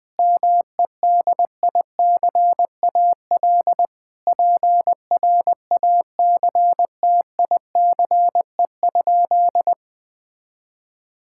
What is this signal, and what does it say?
Morse code, heard as a keyed tone, about 20 wpm, 700 Hz. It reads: MEDICAL PRACTICE?
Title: Sentences from Top 1000 Words - Review Part 5 - 20wpm